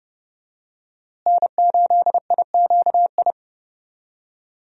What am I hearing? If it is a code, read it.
Morse code, heard as a keyed tone, about 30 wpm, 700 Hz. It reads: N8IQI
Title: USA Call Signs - 5 Characters - Repeated 2x - 30wpm